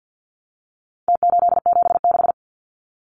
Morse code, heard as a keyed tone, about 49 wpm, 700 Hz. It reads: T876
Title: Numbers - Contest Style - 4-Digit 50wpm